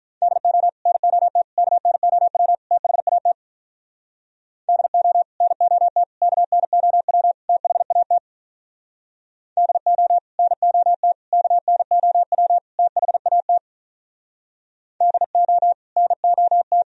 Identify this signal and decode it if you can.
Morse code, heard as a keyed tone, about 46 wpm, 700 Hz. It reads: DO NOT KNOW THAT DO NOT KNOW THAT DO NOT KNOW THAT DO NOT